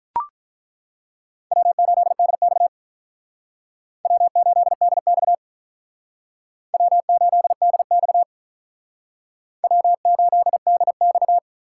Morse code, heard as a keyed tone, about 47 wpm, 700 Hz. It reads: W8DX W8DX W8DX W8DX